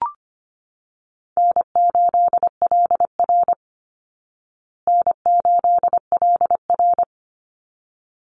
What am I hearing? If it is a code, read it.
Morse code, heard as a keyed tone, about 25 wpm, 700 Hz. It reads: N8LR N8LR
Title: Easy Call Signs Repeated 2x - 25wpm